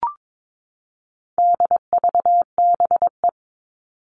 Morse code, heard as a keyed tone, about 22 wpm, 700 Hz. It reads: DVBE
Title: All Cut Numbers - 4-Digit 22wpm